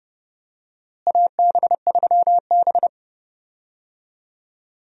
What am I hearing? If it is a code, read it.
Morse code, heard as a keyed tone, about 30 wpm, 700 Hz. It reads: AB3B
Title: USA Call Signs - 4 Characters - Repeated 3x - 30wpm